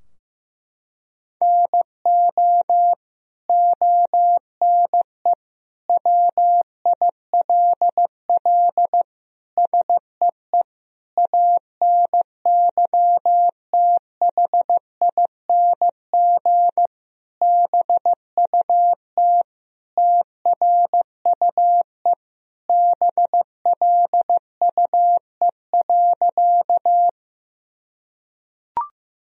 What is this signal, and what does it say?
Morse code, heard as a keyed tone, about 15 wpm, 700 Hz. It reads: NO ONE WILL SEE ANYTHING BUT TRUE BLUE.
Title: Sentences from Top 1000 Words - Review Part 3 - 15wpm